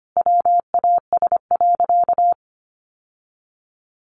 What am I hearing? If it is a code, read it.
Morse code, heard as a keyed tone, about 25 wpm, 700 Hz. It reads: WAS.
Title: Sentences from Top 600 Words 25wpm